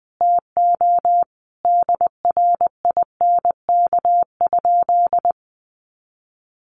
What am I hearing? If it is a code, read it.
Morse code, heard as a keyed tone, about 20 wpm, 700 Hz. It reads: TO DRINK?
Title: Sentences from Top 1500 Words 20wpm